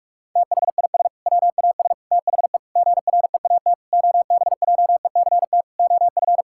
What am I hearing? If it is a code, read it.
Morse code, heard as a keyed tone, about 46 wpm, 700 Hz. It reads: THIS WAS THE GREAT OBJECT OF
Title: Sentences from Top 1000 Words - Review Part 5 - 45wpm